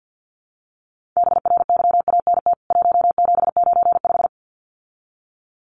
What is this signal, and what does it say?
Morse code, heard as a keyed tone, about 48 wpm, 700 Hz. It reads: BRYANT 1795
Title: CWT Exchange 50wpm